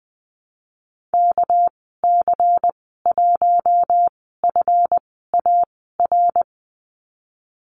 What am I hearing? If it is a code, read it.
Morse code, heard as a keyed tone, about 20 wpm, spaced as for 15 wpm, 700 Hz. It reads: KC1FAR
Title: 200 Lesson 32g - Letter k - Call Signs 20wpm